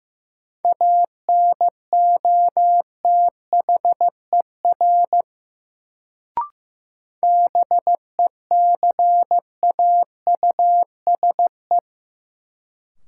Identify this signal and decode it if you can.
Morse code, heard as a keyed tone, about 15 wpm, 700 Hz. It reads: ANOTHER BECAUSE
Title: Sentences from Top 600 Words - New Word Drill - 15wpm